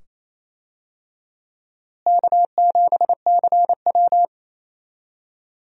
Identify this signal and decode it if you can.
Morse code, heard as a keyed tone, about 28 wpm, 700 Hz. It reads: K7CW
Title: USA Call Signs - 4 Characters - Repeated 2x - 28wpm